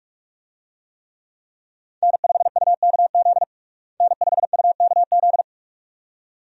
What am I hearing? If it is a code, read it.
Morse code, heard as a keyed tone, about 43 wpm, 700 Hz. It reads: N5UKZ N5UKZ
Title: USA Call Signs - 5 Characters - Repeated 2x - 45wpm